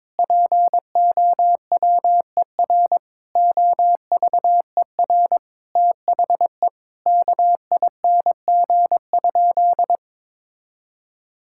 Morse code, heard as a keyed tone, about 22 wpm, 700 Hz. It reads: POWER OVER THE KING?
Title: Sentences from Top 1000 Words - Review Part 6 - 22wpm